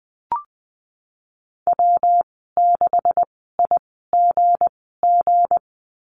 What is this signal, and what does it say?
Morse code, heard as a keyed tone, about 20 wpm, spaced as for 15 wpm, 700 Hz. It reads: W6IGG
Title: 173 Lesson 28g - Number 9 - Call Signs 20wpm